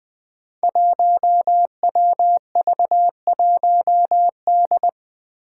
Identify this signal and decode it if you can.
Morse code, heard as a keyed tone, about 20 wpm, 700 Hz. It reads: 1WV1D
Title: Five Challenging Letter-Number Combinations 20wpm